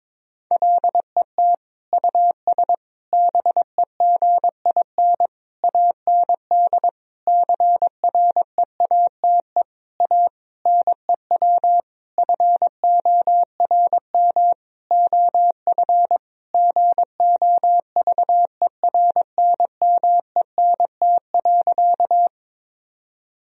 Morse code, heard as a keyed tone, about 22 wpm, 700 Hz. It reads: LET US BEGIN AND CREATE A NEW FORM OF GOVERNMENT.
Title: Sentences from Top 1000 Words - Review Part 3 - 22wpm